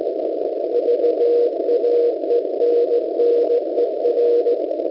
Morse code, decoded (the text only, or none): VAENNER